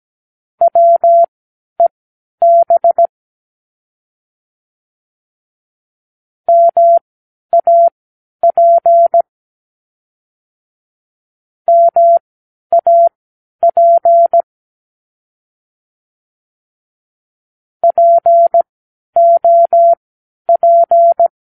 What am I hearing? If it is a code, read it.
Morse code, heard as a keyed tone, about 17 wpm, spaced as for 10 wpm, 680 Hz. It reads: WEB MAP MAP POP
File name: k7qo / Practicewords